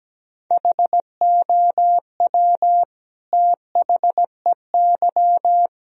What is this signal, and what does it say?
Morse code, heard as a keyed tone, about 17 wpm, 700 Hz. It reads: HOW THEY
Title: Sentences from Top 1000 Words - Review Part 4 - 17wpm